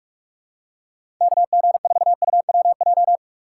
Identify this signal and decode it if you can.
Morse code, heard as a keyed tone, about 42 wpm, 700 Hz. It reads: KG4UWJ
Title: USA Call Signs - 6 Characters - 2x Character Spacing - 45wpm